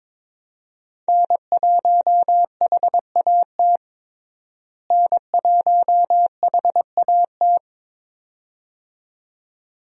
Morse code, heard as a keyed tone, about 21 wpm, 700 Hz. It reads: N1HAT N1HAT
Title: USA Call Signs - 5 Characters - Repeated 2x - 22wpm